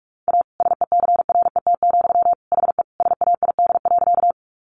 Morse code, heard as a keyed tone, about 44 wpm, 700 Hz. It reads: A SECRET, HE SAID.
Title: Sentences from Top 1200 Words 45wpm